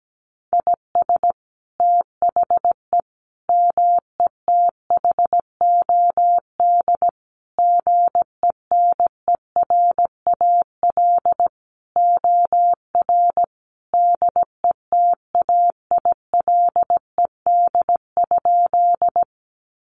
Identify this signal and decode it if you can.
Morse code, heard as a keyed tone, about 17 wpm, 700 Hz. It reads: IS THE METHOD GENERAL OR DETAILED?
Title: Sentences from Top 1500 Words 17wpm